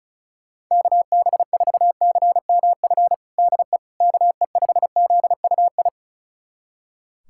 Morse code, heard as a keyed tone, about 36 wpm, 700 Hz. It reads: KB4CMF DE KE5ZUI
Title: USA Callsign DE Callsign 35wpm